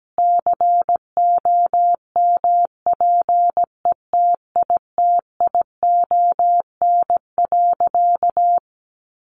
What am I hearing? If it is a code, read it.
Morse code, heard as a keyed tone, about 17 wpm, 700 Hz. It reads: COMPETITION.
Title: Additional Easy Phrases 17wpm